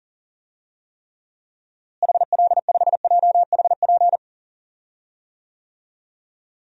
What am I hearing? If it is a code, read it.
Morse code, heard as a keyed tone, about 40 wpm, 700 Hz. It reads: HL5JHP
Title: DXCC-100 Call Signs - 6 Characters - Repeated 2x - 40wpm